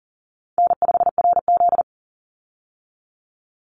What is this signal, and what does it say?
Morse code, heard as a keyed tone, about 40 wpm, 700 Hz. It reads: N5RZ
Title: Call Signs Repeated 2x - 40wpm